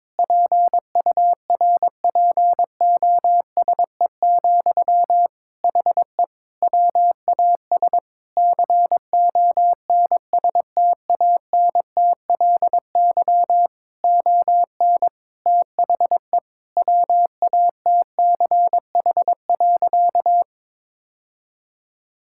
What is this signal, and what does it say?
Morse code, heard as a keyed tone, about 22 wpm, 700 Hz. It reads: PURPOSE, HE WAS CONSTANTLY ON THE WATCH.
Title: Sentences from Top 1600 Words 22wpm